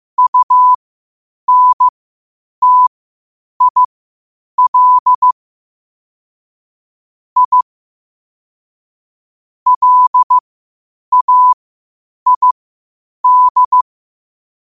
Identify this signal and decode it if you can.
Morse code, heard as a keyed tone, about 15 wpm, spaced as for 8 wpm, 1000 Hz. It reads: UNTIL I LAID
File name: qotdfiles / 8wpm